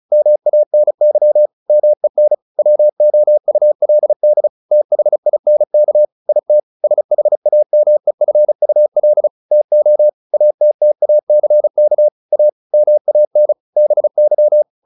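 Morse code, heard as a keyed tone, about 34 wpm, 600 Hz. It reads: MANY MEN WOULD THINK IT SHAMEFUL TO ATTACK A MAN BY
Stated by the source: Morse practice at 35 wpm from CW med Gnister, sending Ken Follett's novel Never in English.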